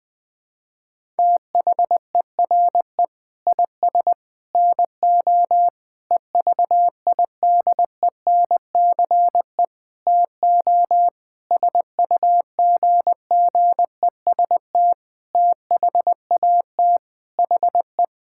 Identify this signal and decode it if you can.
Morse code, heard as a keyed tone, about 20 wpm, 700 Hz. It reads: THERE IS NO EVIDENCE TO SUGGEST THAT HE
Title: Sentences with 7-Word Phrases - 20wpm